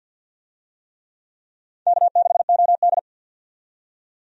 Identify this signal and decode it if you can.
Morse code, heard as a keyed tone, about 53 wpm, 700 Hz. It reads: K6OD
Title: USA Call Signs - 4 Characters - 50wpm